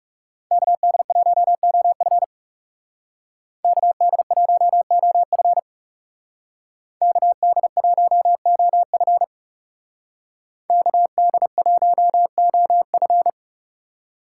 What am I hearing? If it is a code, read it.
Morse code, heard as a keyed tone, about 36 wpm, 700 Hz. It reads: KD1OF KD1OF KD1OF KD1OF
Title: Speed Racing - USA 2x2 Call Signs 30wpm